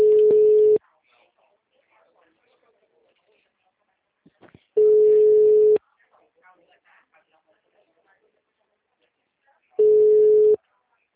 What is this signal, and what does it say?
Ringback tone (Europe, 425 Hz), with speech.